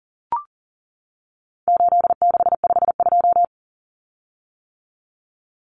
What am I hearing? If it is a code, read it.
Morse code, heard as a keyed tone, about 40 wpm, 700 Hz. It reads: 8652
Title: Common Cut Numbers - 4-Digit 40wpm